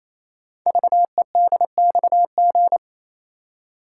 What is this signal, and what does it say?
Morse code, heard as a keyed tone, about 28 wpm, 700 Hz. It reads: VEDXG
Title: Code Groups - 5 Letters - 28wpm